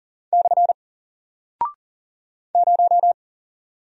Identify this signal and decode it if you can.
Morse code, heard as a keyed tone, about 40 wpm, 700 Hz. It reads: / 0
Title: Single Character (Letter:Number:Punctuation:Prosign) Alt 40wpm